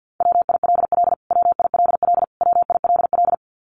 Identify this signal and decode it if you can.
Morse code, heard as a keyed tone, about 49 wpm, 700 Hz. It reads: WILL WILL WILL
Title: Top 500 Words Repeated 50wpm